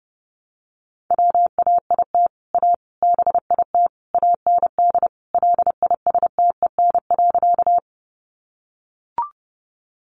Morse code, heard as a keyed tone, about 30 wpm, 700 Hz. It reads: WAIT A BIT AND LISTEN.
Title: Sentences from Top 900 Words 30wpm